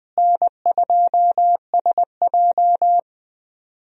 Morse code, heard as a keyed tone, about 20 wpm, 700 Hz. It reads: N2SJ
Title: USA Call Signs - 4 Characters - Repeated 2x - 20wpm